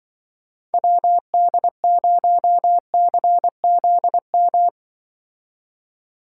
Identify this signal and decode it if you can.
Morse code, heard as a keyed tone, about 24 wpm, 700 Hz. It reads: WD0CZM
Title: Speed Racing - USA 2x3 Call Signs 20wpm